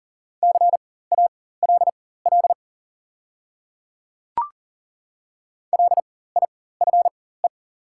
Morse code, heard as a keyed tone, about 40 wpm, spaced as for 19 wpm, 700 Hz. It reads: CALL LIFE
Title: 4x Character Spacing - Top 300 Words - 40wpm